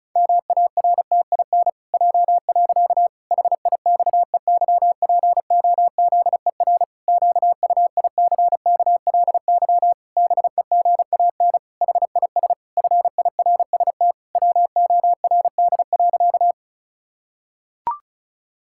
Morse code, heard as a keyed tone, about 36 wpm, 700 Hz. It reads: MARTIN J. HIXEYPOZER QUICKLY BEGAN HIS FIRST WORD.
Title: Pangram Challenge v2 - 35wpm